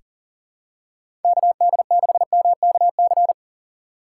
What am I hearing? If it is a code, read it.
Morse code, heard as a keyed tone, about 39 wpm, 700 Hz. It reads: KD6MKC
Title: USA Call Signs - 6 Characters - 6x Character Spacing - 40wpm